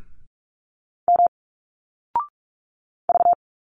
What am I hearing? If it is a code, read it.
Morse code, heard as a keyed tone, about 46 wpm, 700 Hz. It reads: M V
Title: Single Letters 45wpm